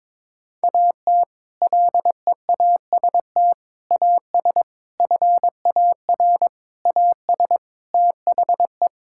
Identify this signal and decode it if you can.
Morse code, heard as a keyed tone, about 22 wpm, 700 Hz. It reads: AT LEAST AS FAR AS THE
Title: Sentences from Top 1000 Words - Review Part 6 - 22wpm